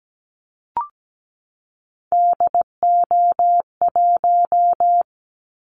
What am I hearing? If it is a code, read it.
Morse code, heard as a keyed tone, about 17 wpm, 700 Hz. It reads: DO1
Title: Callsign Prefixes for Top 50 DXCC - 17wpm